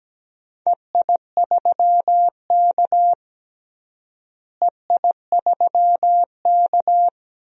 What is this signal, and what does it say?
Morse code, heard as a keyed tone, about 17 wpm, 700 Hz. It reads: EI3K EI3K
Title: DXCC-100 Call Signs - 4 Characters - Repeated 2x - 17wpm